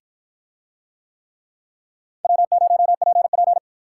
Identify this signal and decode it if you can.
Morse code, heard as a keyed tone, about 48 wpm, 700 Hz. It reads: W0PP